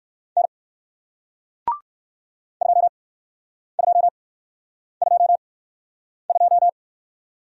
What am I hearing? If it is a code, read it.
Morse code, heard as a keyed tone, about 63 wpm, spaced as for 9 wpm, 700 Hz. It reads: N 2222